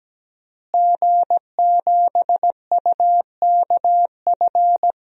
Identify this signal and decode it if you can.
Morse code, heard as a keyed tone, about 17 wpm, 700 Hz. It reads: G7UKF
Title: DXCC-100 Call Signs - 5 Characters - Repeated 2x - 17wpm